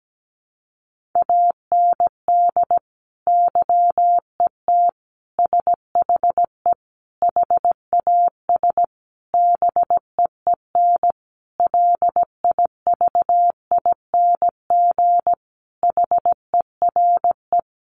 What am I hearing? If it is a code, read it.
Morse code, heard as a keyed tone, about 17 wpm, 700 Hz. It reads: AND YET SHE HAS BEEN LIVING HERE
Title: Sentences from Top 1100 Words 17wpm